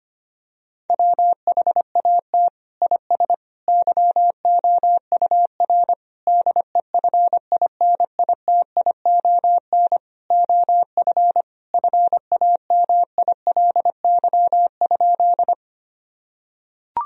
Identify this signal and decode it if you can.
Morse code, heard as a keyed tone, about 25 wpm, 700 Hz. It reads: WHAT IS YOUR DEFINITION OF FAMILY?
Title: Sentences from Top 1000 Words - Review Part 4 - 25wpm